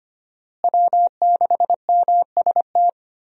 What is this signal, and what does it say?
Morse code, heard as a keyed tone, about 25 wpm, 700 Hz. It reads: W6MST
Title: USA Call Signs - 5 Characters - Repeated 3x - 25wpm